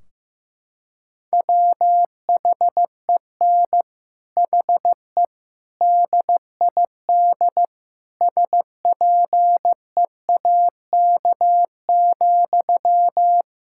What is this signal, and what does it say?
Morse code, heard as a keyed tone, about 15 wpm, 700 Hz. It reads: WHEN HE DID SPEAK,
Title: Sentences from Top 1600 Words 15wpm